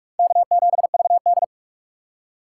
Morse code, heard as a keyed tone, about 46 wpm, 700 Hz. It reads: K7VD
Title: USA Call Signs - 2x Character Spacing - 45wpm